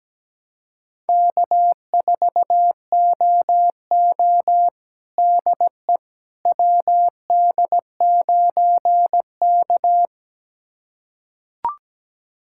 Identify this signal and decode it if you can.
Morse code, heard as a keyed tone, about 17 wpm, 700 Hz. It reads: K4OO DE WD9K
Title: USA Callsign DE Callsign - 4 Characters - 17wpm